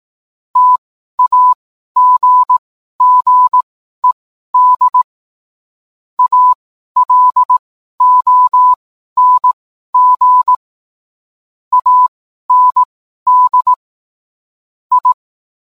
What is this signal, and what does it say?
Morse code, heard as a keyed tone, about 18 wpm, spaced as for 12 wpm, 1000 Hz. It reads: TAGGED ALONG AND I